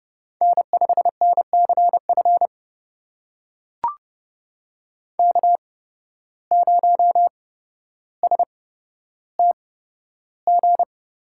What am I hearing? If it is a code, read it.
Morse code, heard as a keyed tone, about 31 wpm, spaced as for 13 wpm, 700 Hz. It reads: N5NCF K0STG
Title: USA Call Signs - 8x Character Spacing - 30wpm